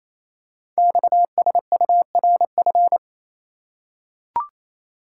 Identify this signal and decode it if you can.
Morse code, heard as a keyed tone, about 27 wpm, 700 Hz. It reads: XSURF
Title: Code Groups - 5 Letters - 28wpm